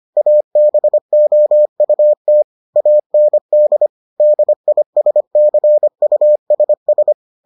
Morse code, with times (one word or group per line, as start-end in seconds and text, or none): 0.2-2.4 ABOUT
2.8-3.9 AND
4.2-7.1 DISCUSS